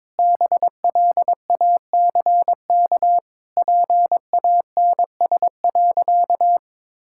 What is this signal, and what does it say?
Morse code, heard as a keyed tone, about 22 wpm, 700 Hz. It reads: BLACK PANS.
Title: Pangram Challenge v2 - 22wpm